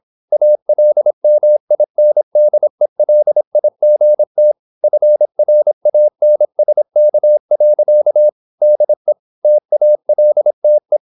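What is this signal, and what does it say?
Morse code, heard as a keyed tone, about 26 wpm, 600 Hz. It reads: ALMINDELIGT FRANSK. DE TALTE
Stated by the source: High-speed (QRQ) Morse at 26 wpm.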